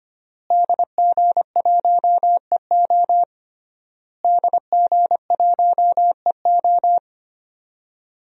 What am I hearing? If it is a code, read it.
Morse code, heard as a keyed tone, about 25 wpm, 700 Hz. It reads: DG1EO DG1EO
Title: DXCC-100 Call Signs - 5 Characters - Repeated 3x - 25wpm